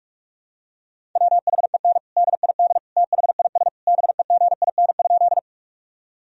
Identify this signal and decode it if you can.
Morse code, heard as a keyed tone, about 48 wpm, 700 Hz. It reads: WHEN DID THIS BEGIN?
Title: Sentences from Top 1000 Words - Review Part 1 - 45wpm